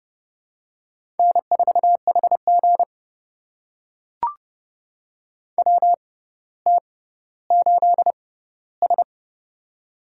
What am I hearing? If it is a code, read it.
Morse code, heard as a keyed tone, about 30 wpm, spaced as for 15 wpm, 700 Hz. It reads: N4HG WT8S
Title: USA Call Signs - 4 Characters - 6x Character Spacing - 30wpm